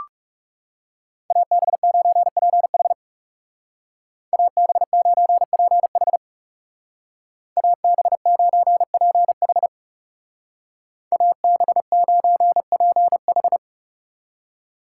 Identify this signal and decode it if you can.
Morse code, heard as a keyed tone, about 39 wpm, 700 Hz. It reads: AB9PH AB9PH AB9PH AB9PH